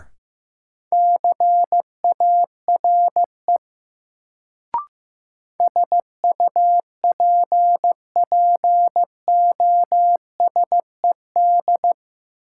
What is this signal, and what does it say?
Morse code, heard as a keyed tone, about 15 wpm, 700 Hz. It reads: CARE SUPPOSED
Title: Sets of 3 Words - New Word Drill - 15wpm